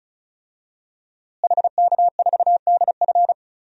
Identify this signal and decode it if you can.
Morse code, heard as a keyed tone, about 36 wpm, 700 Hz. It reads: HK4DF
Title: DXCC-100 Call Signs - 5 Characters - Repeated 2x - 35wpm